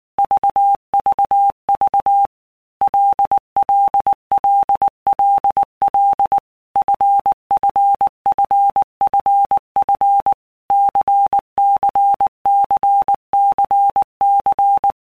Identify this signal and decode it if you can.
Morse code, heard as a keyed tone, about 19 wpm, 800 Hz. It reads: VVV LLLLL FFFFF CCCCC
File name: Basic Letter Groups By Blocks / Groups@18 / LFCYQ@18